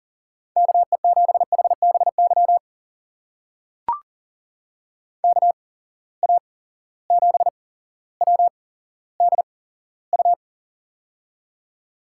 Morse code, heard as a keyed tone, about 41 wpm, spaced as for 17 wpm, 700 Hz. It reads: KE7HBY KA7WDU
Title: USA Call Signs - 6 Characters - 8x Character Spacing - 40wpm